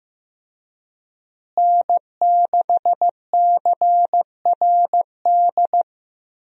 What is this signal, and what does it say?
Morse code, heard as a keyed tone, about 15 wpm, 700 Hz. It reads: N6CRD
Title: USA Call Signs - 5 Characters - Repeated 3x - 15wpm